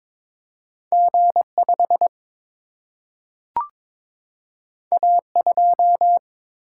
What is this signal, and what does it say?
Morse code, heard as a keyed tone, about 22 wpm, 700 Hz. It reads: G 5 A 2
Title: Single-Letter Single-Number 22wpm